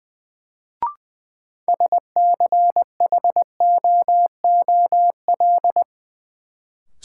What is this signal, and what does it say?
Morse code, heard as a keyed tone, about 20 wpm, 700 Hz. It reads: SCHOOL
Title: Sets of 2 Words - New Word Drill - 20wpm